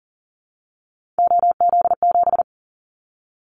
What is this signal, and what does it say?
Morse code, heard as a keyed tone, about 40 wpm, 700 Hz. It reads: OZ7
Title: Callsign Prefixes for Top 50 DXCC - 40wpm